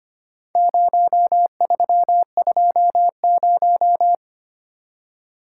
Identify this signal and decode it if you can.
Morse code, heard as a keyed tone, about 25 wpm, 700 Hz. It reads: 0320
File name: practice / 25wpm / Four Letter-Number Combinations - 25wpm